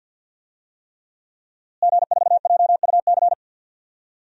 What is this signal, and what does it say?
Morse code, heard as a keyed tone, about 52 wpm, 700 Hz. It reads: G4JUC
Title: DXCC-100 Call Signs - 5 Characters - Repeated 3x - 50wpm